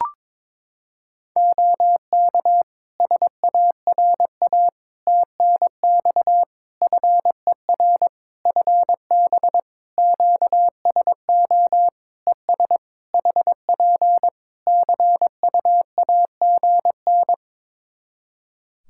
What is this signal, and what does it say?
Morse code, heard as a keyed tone, about 22 wpm, 700 Hz. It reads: OK SARA TNX FER FB QSO ES HP CUAGN
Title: Practice QSO - Exchange 4 - Ending - 22wpm